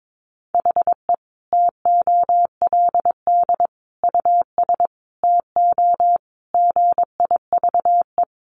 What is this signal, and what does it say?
Morse code, heard as a keyed tone, about 22 wpm, 700 Hz. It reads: HE TOLD US TO GIVE
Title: Sentences from Top 1000 Words 22wpm